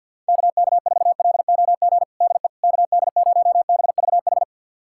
Morse code, heard as a keyed tone, about 53 wpm, 700 Hz. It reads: KK4LOG DE KD0BVH
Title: USA Callsign DE Callsign - 6 Characters - 50wpm